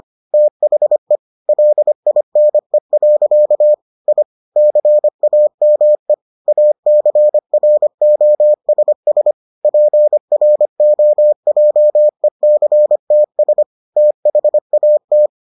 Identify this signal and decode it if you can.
Morse code, heard as a keyed tone, about 25 wpm, 600 Hz. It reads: THE LINE. I CAME ACROSS PROJECTS THAT